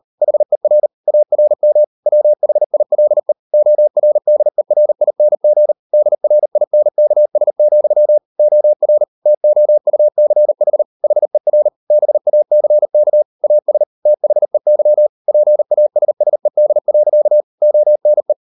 Morse code, heard as a keyed tone, about 38 wpm, 600 Hz. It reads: HER ARM WHILE ORDERING DRINKS, OR TOUCH HER BACK AS THEY PASSED. ONE